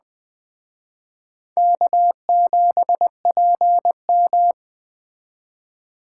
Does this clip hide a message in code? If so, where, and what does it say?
Yes: Morse code from 1.6 to 4.5 s, heard as a keyed tone, about 20 wpm, 700 Hz. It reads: K7PM